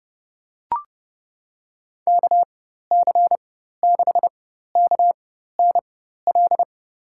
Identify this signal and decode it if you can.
Morse code, heard as a keyed tone, about 32 wpm, spaced as for 14 wpm, 700 Hz. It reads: KC6KNL
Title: USA Call Signs - 6 Characters - 4x Character Spacing - 30wpm